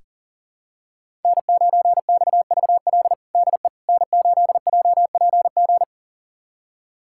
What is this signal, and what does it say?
Morse code, heard as a keyed tone, about 40 wpm, 700 Hz. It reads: N9XVL DE N8JPG
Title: USA Callsign DE Callsign - 5 Characters - 40wpm